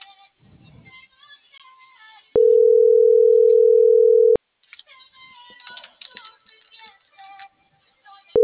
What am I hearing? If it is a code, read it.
Ringback tone (North America), 440 + 480 Hz.